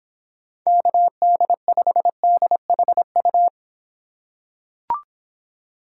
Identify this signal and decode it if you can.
Morse code, heard as a keyed tone, about 26 wpm, 700 Hz. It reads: KD5DHU